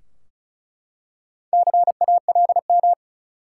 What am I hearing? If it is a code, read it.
Morse code, heard as a keyed tone, about 36 wpm, 700 Hz. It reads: CALM
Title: Four Letter Words - 35wpm